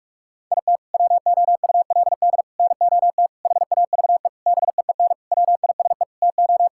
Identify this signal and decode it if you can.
Morse code, heard as a keyed tone, about 46 wpm, 700 Hz. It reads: IT WOULD NOT HAVE BEEN WISE TO
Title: Sentences from Top 1600 Words 45wpm